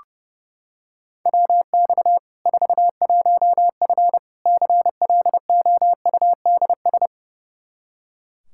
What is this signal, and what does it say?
Morse code, heard as a keyed tone, about 30 wpm, 700 Hz. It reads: WX 41F CLOUDS
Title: Practice QSO - Exchange 2 - Info - 30wpm